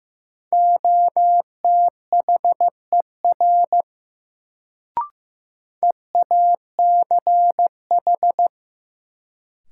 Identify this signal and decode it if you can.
Morse code, heard as a keyed tone, about 15 wpm, 700 Hz. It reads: OTHER EACH
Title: Sets of 2 Words - New Word Drill - 15wpm